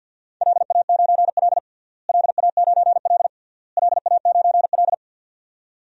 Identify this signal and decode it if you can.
Morse code, heard as a keyed tone, about 52 wpm, 700 Hz. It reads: LA9L LA9L LA9L
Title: DXCC-100 Call Signs - 4 Characters - Repeated 3x - 50wpm